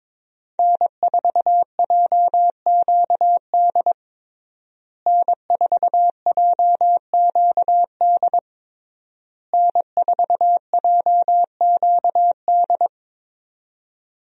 Morse code, heard as a keyed tone, about 22 wpm, 700 Hz. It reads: N4JQD N4JQD N4JQD
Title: USA Call Signs - 5 Characters - Repeated 3x - 22wpm